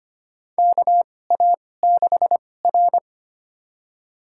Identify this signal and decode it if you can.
Morse code, heard as a keyed tone, about 25 wpm, spaced as for 18 wpm, 700 Hz. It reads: KA6R
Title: USA Call Signs - 2x Character Spacing - 25wpm